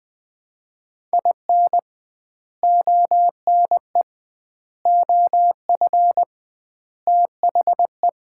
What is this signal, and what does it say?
Morse code, heard as a keyed tone, about 21 wpm, 700 Hz. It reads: IN ONE OF THE
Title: 2x Word-Spacing - Sets of 4 Words - 20wpm